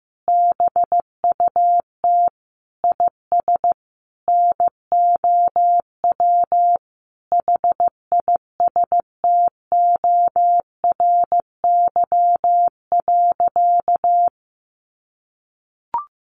Morse code, heard as a keyed tone, about 15 wpm, 700 Hz. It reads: BUT IS NOW HISTORY.
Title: Sentences from Top 1500 Words 15wpm